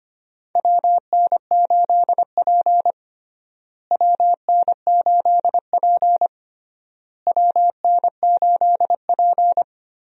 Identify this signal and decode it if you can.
Morse code, heard as a keyed tone, about 25 wpm, 700 Hz. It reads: WN8P WN8P WN8P
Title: USA Call Signs - 4 Characters - Repeated 3x - 25wpm